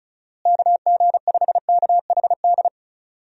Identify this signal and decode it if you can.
Morse code, heard as a keyed tone, about 36 wpm, 700 Hz. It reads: KG5KHD
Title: USA Call Signs - 6 Characters - Repeated 3x - 35wpm